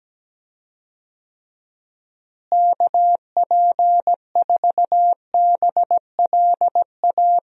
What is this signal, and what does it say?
Morse code, heard as a keyed tone, about 17 wpm, 700 Hz. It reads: KP4BLA
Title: DXCC-100 Call Signs - 6 Characters - Repeated 3x - 17wpm